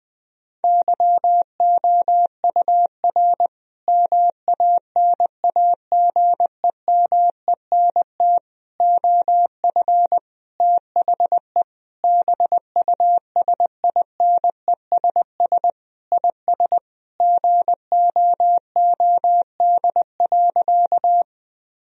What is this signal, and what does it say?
Morse code, heard as a keyed tone, about 20 wpm, 700 Hz. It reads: YOUR MANAGEMENT OF THE BUSINESS IS GOOD.
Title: Sentences from Top 1000 Words - Review Part 4 - 20wpm